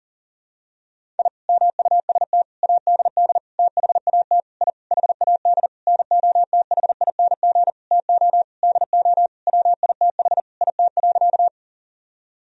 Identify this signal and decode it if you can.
Morse code, heard as a keyed tone, about 40 wpm, 700 Hz. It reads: I MUST ADD THAT I HAD NOTHING TO DO WITH IT.
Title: Sentences from Top 1000 Words - Review Part 1 - 40wpm